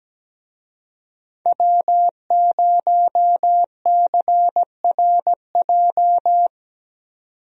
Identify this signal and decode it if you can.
Morse code, heard as a keyed tone, about 17 wpm, 700 Hz. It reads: W0CRJ
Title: USA Call Signs - 5 Characters - 17wpm